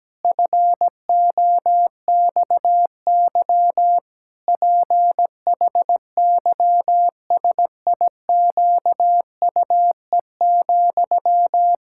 Morse code, heard as a keyed tone, about 17 wpm, 700 Hz. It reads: FOXY PHYSIQUE,
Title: Pangram Challenge v2 - 17wpm